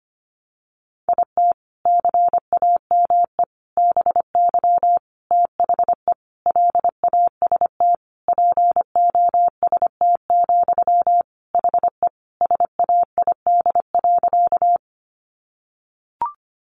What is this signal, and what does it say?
Morse code, heard as a keyed tone, about 25 wpm, 700 Hz. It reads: IT CAME BY THE LAST POST, HE SAID.
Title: Sentences from Top 1000 Words 25wpm